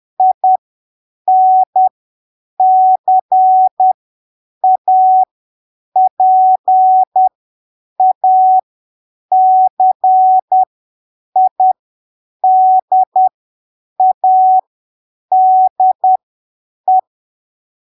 Morse code, heard as a keyed tone, about 10 wpm, spaced as for 7 wpm, 750 Hz. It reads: INCAPACIDADE
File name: arquivos / texto / texto 24 10ppm